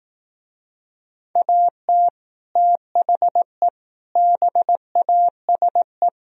Morse code, heard as a keyed tone, about 18 wpm, 700 Hz. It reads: AT THE BASE
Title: Speed-Racing - Sets of 5 Words 15wpm